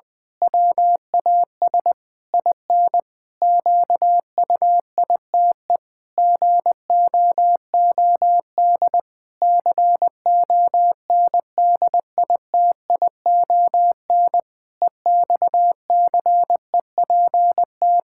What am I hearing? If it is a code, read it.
Morse code, heard as a keyed tone, about 20 wpm, 700 Hz. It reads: WAS IN QUITE GOOD CONDITION EXCEPT
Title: Pangram Challenge v2 - 20wpm